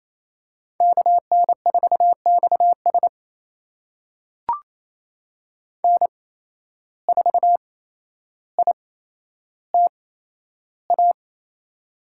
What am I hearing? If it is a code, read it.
Morse code, heard as a keyed tone, about 28 wpm, spaced as for 12 wpm, 700 Hz. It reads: KN4XS N4ITA
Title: USA Call Signs - 5 Characters - 8x Character Spacing - 28wpm